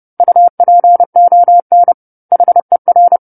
Morse code, heard as a keyed tone, about 30 wpm, 700 Hz. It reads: UPON HER
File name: 10-Chapter-10 30 WPM